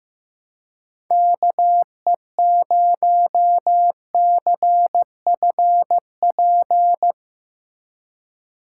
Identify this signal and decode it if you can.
Morse code, heard as a keyed tone, about 15 wpm, 700 Hz. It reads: KE0CFP
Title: USA Call Signs - 6 Characters - 4x Character Spacing - 15wpm